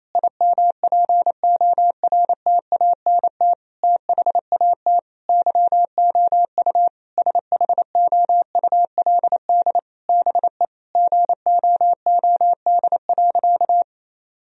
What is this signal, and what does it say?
Morse code, heard as a keyed tone, about 28 wpm, 700 Hz. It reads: IMPORTANT THAT YOU SHOULD BE GOOD.
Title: Sentences from Top 1000 Words - Review Part 1 - 28wpm